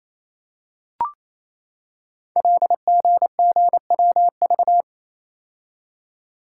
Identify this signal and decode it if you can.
Morse code, heard as a keyed tone, about 27 wpm, 700 Hz. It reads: LGGWV
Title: Code Groups - 5 Letters - 28wpm